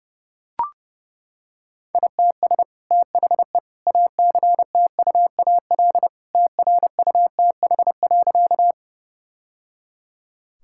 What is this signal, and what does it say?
Morse code, heard as a keyed tone, about 30 wpm, 700 Hz. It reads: ITS THE ACTUAL TRUTH.
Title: Sentences from Top 1000 Words - Review Part 5 - 30wpm